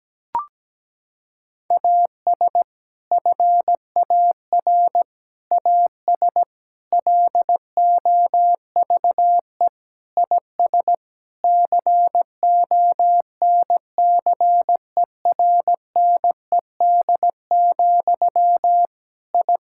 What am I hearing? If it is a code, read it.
Morse code, heard as a keyed tone, about 17 wpm, 700 Hz. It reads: AS FAR AS LOVE IS CONCERNED, I
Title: Sentences from Top 1000 Words - Review Part 6 - 17wpm